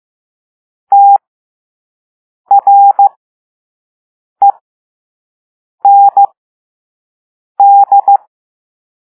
Morse code, heard as a keyed tone, about 15 wpm, spaced as for 5 wpm, 800 Hz. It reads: TREND
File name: Morse / Level3-All-B